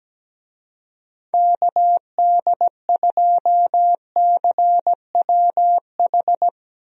Morse code, heard as a keyed tone, about 17 wpm, 700 Hz. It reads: KD2CWH
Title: USA Call Signs - 6 Characters - Repeated 3x - 17wpm